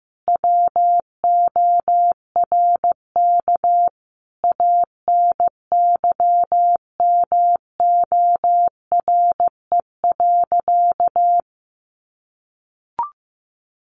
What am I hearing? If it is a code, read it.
Morse code, heard as a keyed tone, about 15 wpm, 700 Hz. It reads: WORK ANYMORE.
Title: Additional Easy Phrases 15wpm